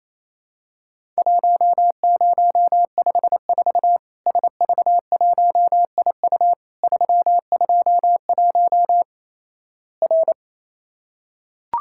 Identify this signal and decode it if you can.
Morse code, heard as a keyed tone, about 28 wpm, 700 Hz. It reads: 1054 SV1IU 321 R
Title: Worked All Europe DX Contest - QTC Exchange - 28wpm